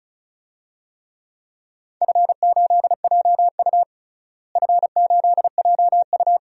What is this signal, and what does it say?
Morse code, heard as a keyed tone, about 36 wpm, 700 Hz. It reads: F8JU F8JU
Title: DXCC-100 Call Signs - 4 Characters - Repeated 2x - 35wpm